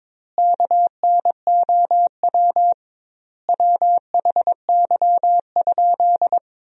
Morse code, heard as a keyed tone, about 22 wpm, 700 Hz. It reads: KNOW WHY?
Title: 2x Word-Spacing - Sentences from Top 200 Words - 22wpm